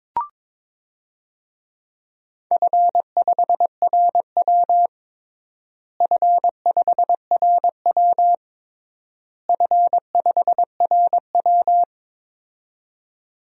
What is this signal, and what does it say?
Morse code, heard as a keyed tone, about 22 wpm, 700 Hz. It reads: F5RW F5RW F5RW
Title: DXCC-100 Call Signs - 4 Characters - Repeated 3x - 22wpm